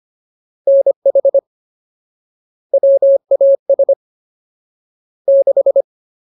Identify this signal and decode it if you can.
Morse code, heard as a keyed tone, about 25 wpm, 550 Hz. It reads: NH WAS 6
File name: Ultimate Word List / 25WPM Ultimate